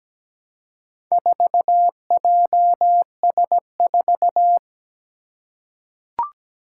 Morse code, heard as a keyed tone, about 17 wpm, 700 Hz. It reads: 4JS4
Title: Four Challenging Letter-Number Combinations 17wpm